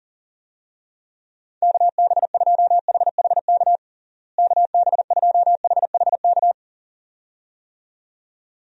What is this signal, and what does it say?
Morse code, heard as a keyed tone, about 40 wpm, 700 Hz. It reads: KB2HHK KB2HHK
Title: USA Call Signs - 6 Characters - Repeated 2x - 40wpm